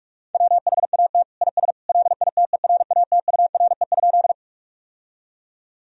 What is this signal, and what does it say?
Morse code, heard as a keyed tone, about 46 wpm, 700 Hz. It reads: WHAT IS LITERATURE?
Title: Sentences from Top 1000 Words - Review Part 3 - 45wpm